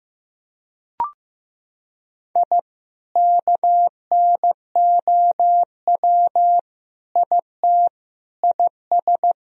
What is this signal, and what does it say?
Morse code, heard as a keyed tone, about 16 wpm, 700 Hz. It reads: I KNOW IT IS
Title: Sentences from Top 1000 Words - Review Part 3 - 15wpm